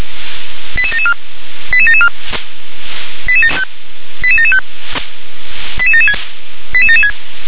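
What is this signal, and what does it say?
Five-tone selective call (ZVEI), heard as quick runs of five tones, tones of about 70 ms. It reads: 88774 88774 88775 88776 88776
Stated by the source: ZVEI five-tone selective call 88771 received on SDR.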